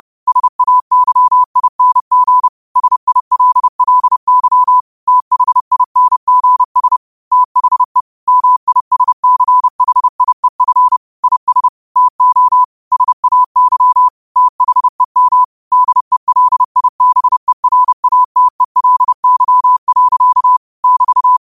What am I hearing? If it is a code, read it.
Morse code, heard as a keyed tone, about 30 wpm, 1000 Hz. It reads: SAYING SILLY THINGS THE MISCHIEF IS TO SAY THEM DELIBERATELY. <BT>
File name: qotdfiles / 30wpm